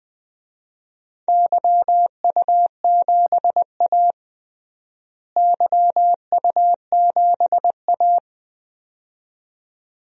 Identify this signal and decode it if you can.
Morse code, heard as a keyed tone, about 20 wpm, 700 Hz. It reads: YU7A YU7A
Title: DXCC-100 Call Signs - 4 Characters - Repeated 2x - 20wpm